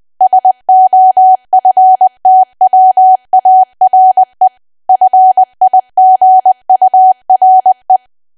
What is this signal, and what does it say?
Morse code, heard as a keyed tone, about 20 wpm, 750 Hz. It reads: SOFTWARE FIGURE
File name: Archive / 20 WPM / 20WPM